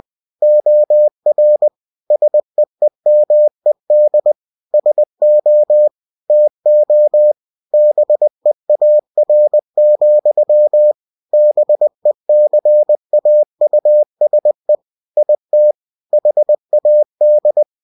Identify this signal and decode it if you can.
Morse code, heard as a keyed tone, about 20 wpm, 600 Hz. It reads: OR SEEMED SO TO BEAR, BECAUSE IT HAD
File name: cw-books / winnie-the-pooh / winnie-20wpm-page-0028